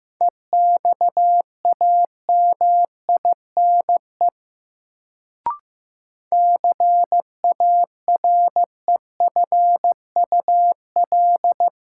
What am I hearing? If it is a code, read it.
Morse code, heard as a keyed tone, about 15 wpm, 700 Hz. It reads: EXAMINE CAREFUL
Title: Sentences from Top 1600 Words - New Word Drill Alt - 15wpm